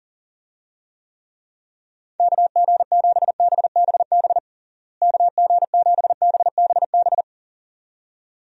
Morse code, heard as a keyed tone, about 40 wpm, 700 Hz. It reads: KG7BBB KG7BBB
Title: USA Call Signs - 6 Characters - Repeated 2x - 40wpm